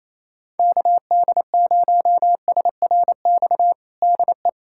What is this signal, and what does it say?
Morse code, heard as a keyed tone, about 28 wpm, 700 Hz. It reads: KD0SRX DE
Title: USA Callsign DE Callsign - 6 Characters - 28wpm